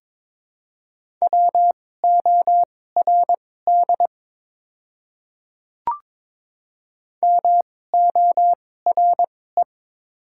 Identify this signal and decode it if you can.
Morse code, heard as a keyed tone, about 21 wpm, spaced as for 16 wpm, 700 Hz. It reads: WORD MORE
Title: 2x Character Spacing - Top 100 Words - 22wpm